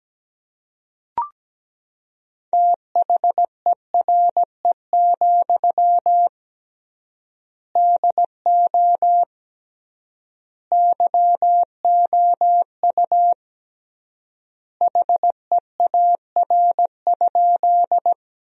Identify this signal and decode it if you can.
Morse code, heard as a keyed tone, about 17 wpm, 700 Hz. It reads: THERE, DO YOU HEAR?
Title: 3x Word-Spacing - Sentences from Top 300 Words - 17wpm